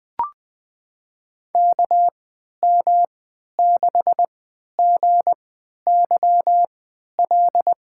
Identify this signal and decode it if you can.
Morse code, heard as a keyed tone, about 20 wpm, spaced as for 11 wpm, 700 Hz. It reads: KM6GYL
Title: USA Call Signs - 3x Character Spacing - 20wpm